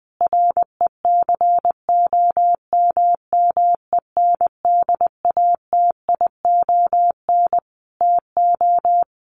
Morse code, heard as a keyed tone, about 20 wpm, 700 Hz. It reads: RECOMMENDATION TO
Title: Sentences from Top 1400 Words 20wpm